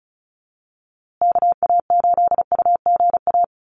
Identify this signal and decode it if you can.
Morse code, heard as a keyed tone, about 34 wpm, 700 Hz. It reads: KA8UGA
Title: USA Call Signs Repeated 2x 35wpm